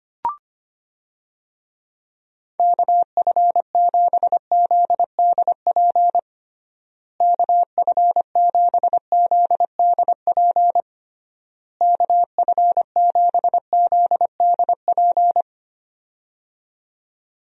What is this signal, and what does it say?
Morse code, heard as a keyed tone, about 25 wpm, 700 Hz. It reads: KF7ZDP KF7ZDP KF7ZDP
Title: USA Call Signs - 6 Characters - Repeated 3x - 25wpm